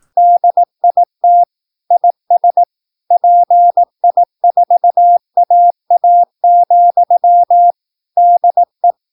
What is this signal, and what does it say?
Morse code, heard as a keyed tone, about 18 wpm, 700 Hz. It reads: DIT IS PI4AA, DE
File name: PI4AA Uitzending